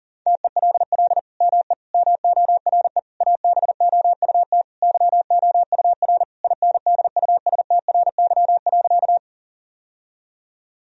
Morse code, heard as a keyed tone, about 37 wpm, 700 Hz. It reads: TELL ME MORE ABOUT YOUR INDUSTRY.
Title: Sentences from Top 1000 Words - Review Part 4 - 40wpm